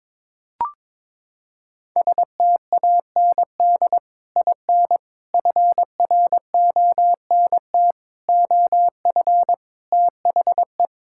Morse code, heard as a keyed tone, about 22 wpm, 700 Hz. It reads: STAND IN FRONT OF THE
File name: practice / 22wpm / Sentences with 3-Word Phrases - 22wpm